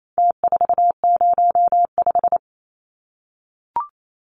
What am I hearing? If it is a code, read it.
Morse code, heard as a keyed tone, about 28 wpm, 700 Hz. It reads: T405
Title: Numbers - Contest Style - 4-Digit 28wpm